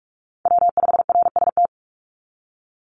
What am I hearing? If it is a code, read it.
Morse code, heard as a keyed tone, about 44 wpm, 700 Hz. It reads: W5RST
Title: NA SOTA Chaser Callsigns 45wpm